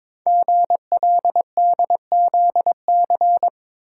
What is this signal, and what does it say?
Morse code, heard as a keyed tone, about 22 wpm, 700 Hz. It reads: GLDZC
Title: Code Groups - 5 Letters - 22wpm